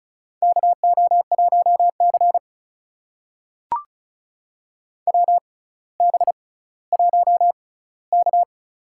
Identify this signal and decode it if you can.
Morse code, heard as a keyed tone, about 34 wpm, spaced as for 18 wpm, 700 Hz. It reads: KO1C WB1K
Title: USA Call Signs - 4 Characters - 6x Character Spacing - 35wpm